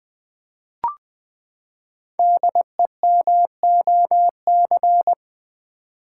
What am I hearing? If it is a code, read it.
Morse code, heard as a keyed tone, about 20 wpm, 700 Hz. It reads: DEMOC
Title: Build A Word - 5 Letters At A Time - 10-Letter Words - 20wpm